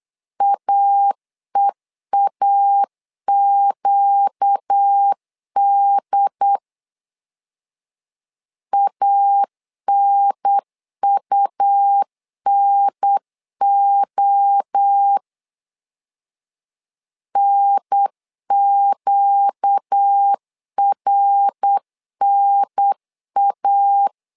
Morse code, heard as a keyed tone, about 8 wpm, 800 Hz. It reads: AEAQD ANUNO NQRNA
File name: 4 Q 8wpm